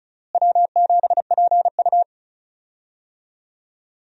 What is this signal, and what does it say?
Morse code, heard as a keyed tone, about 36 wpm, 700 Hz. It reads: W7PU
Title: USA Call Signs - 4 Characters - Repeated 2x - 35wpm